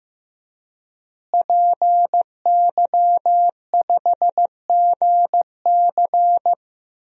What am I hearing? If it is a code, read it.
Morse code, heard as a keyed tone, about 15 wpm, 700 Hz. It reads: PY5GC
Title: DXCC-100 Call Signs - 5 Characters - Repeated 2x - 15wpm